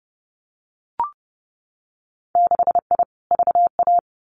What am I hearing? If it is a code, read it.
Morse code, heard as a keyed tone, about 30 wpm, 700 Hz. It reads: 6I VA
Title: Winter Field Day Exchanges Encore 30wpm